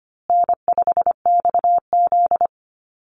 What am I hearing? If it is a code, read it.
Morse code, heard as a keyed tone, about 25 wpm, 700 Hz. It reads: N5XZ
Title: Easy Call Signs Repeated 3x - 25wpm